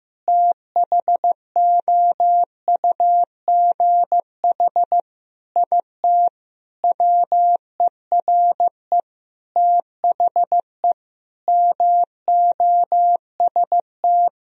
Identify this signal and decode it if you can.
Morse code, heard as a keyed tone, about 15 wpm, 700 Hz. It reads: THOUGH IT WERE THE MOST
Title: Sentences from Top 1000 Words - Review Part 3 - 15wpm